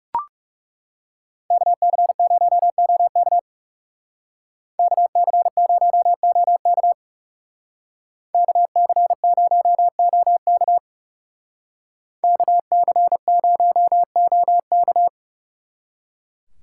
Morse code, heard as a keyed tone, about 39 wpm, 700 Hz. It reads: KC0OK KC0OK KC0OK KC0OK